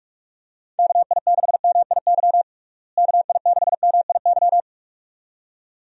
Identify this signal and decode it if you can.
Morse code, heard as a keyed tone, about 46 wpm, 700 Hz. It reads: KI6MIY KI6MIY
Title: USA Call Signs - 6 Characters - Repeated 2x - 45wpm